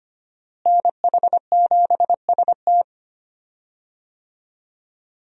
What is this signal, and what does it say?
Morse code, heard as a keyed tone, about 25 wpm, 700 Hz. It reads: NH7ST
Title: USA Call Signs - 5 Characters - Repeated 3x - 25wpm